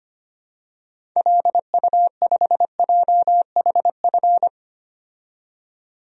Morse code, heard as a keyed tone, about 25 wpm, 700 Hz. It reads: LU5JHF
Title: DXCC-100 Call Signs - 6 Characters - 25wpm